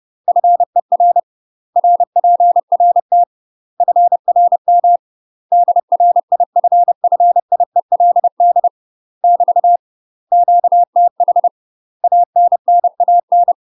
Morse code, heard as a keyed tone, about 30 wpm, 700 Hz. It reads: FER RPRT FRM DRIFFIELD <BT> QTH ANNAN
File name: UK MASTER QSOs TRAINING / 30WPM / UK MASTER QSOs 0017